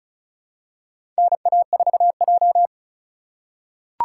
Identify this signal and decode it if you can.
Morse code, heard as a keyed tone, about 36 wpm, 700 Hz. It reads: NA4J
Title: USA Call Signs - 4 Characters - 4x Character Spacing - 35wpm